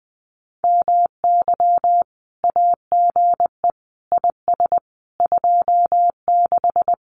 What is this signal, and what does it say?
Morse code, heard as a keyed tone, about 20 wpm, 700 Hz. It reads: MY AGE IS 26
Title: Common QSO Phrases Encore 20wpm